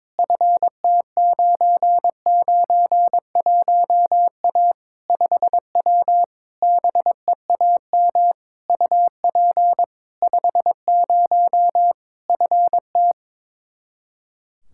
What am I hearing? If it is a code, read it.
Morse code, heard as a keyed tone, about 22 wpm, 700 Hz. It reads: FT991A 5W BEAM UP 50 FT
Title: Practice QSO - Exchange 2 - Info - 22wpm